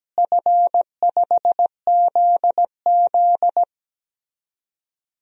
Morse code, heard as a keyed tone, about 17 wpm, 700 Hz. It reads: F5ZZ
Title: DXCC-100 Call Signs - 4 Characters - Repeated 3x - 17wpm